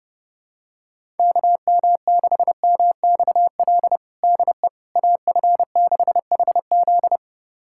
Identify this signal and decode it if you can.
Morse code, heard as a keyed tone, about 32 wpm, 700 Hz. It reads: KM6MXL DE AF6HZ
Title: USA Callsign DE Callsign 30wpm